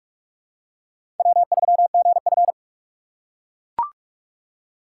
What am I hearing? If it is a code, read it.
Morse code, heard as a keyed tone, about 41 wpm, 700 Hz. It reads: W3GF
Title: USA Call Signs - 6x Character Spacing - 45wpm